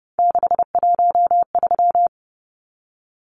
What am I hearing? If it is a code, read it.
Morse code, heard as a keyed tone, about 30 wpm, 700 Hz. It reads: 613
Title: Numbers - 3-Digit 30wpm